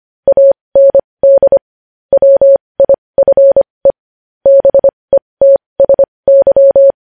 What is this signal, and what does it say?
Morse code, heard as a keyed tone, about 25 wpm, spaced as for 20 wpm, 550 Hz. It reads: AND WIFE BETSY
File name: podcast / News headlines-1x-25x20